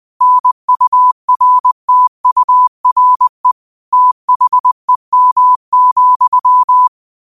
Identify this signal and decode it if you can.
Morse code, heard as a keyed tone, about 20 wpm, 1000 Hz. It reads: NURTURE THEM,
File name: qotdfiles / 20wpm